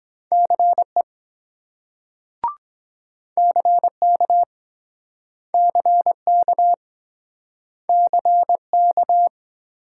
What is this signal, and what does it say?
Morse code, heard as a keyed tone, about 21 wpm, 700 Hz. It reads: CE CK CK CK
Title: Speed-Racing Multi-letter Phonemes - 17wpm